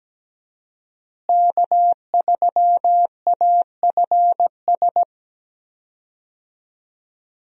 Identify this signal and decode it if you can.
Morse code, heard as a keyed tone, about 17 wpm, 700 Hz. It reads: K3AFS
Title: USA Call Signs - 5 Characters - Repeated 2x - 17wpm